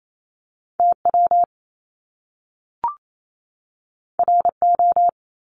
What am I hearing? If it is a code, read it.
Morse code, heard as a keyed tone, about 28 wpm, 700 Hz. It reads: TW RO
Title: Top Two Letter Combinations in Words 28wpm